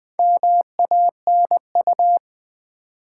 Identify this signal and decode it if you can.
Morse code, heard as a keyed tone, about 20 wpm, 700 Hz. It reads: MANU
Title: Build A Word - 4 Letters At A Time - 10-Letter Words - 20wpm